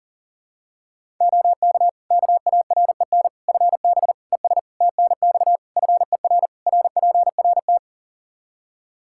Morse code, heard as a keyed tone, about 40 wpm, 700 Hz. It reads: OK KAREN FB ES TNX FER RPRT
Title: Practice QSO - Exchange 2 - Info - 40wpm